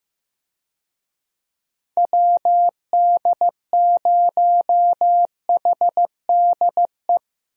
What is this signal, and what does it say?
Morse code, heard as a keyed tone, about 15 wpm, 700 Hz. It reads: WD0HDE
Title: USA Call Signs - 6 Characters - 15wpm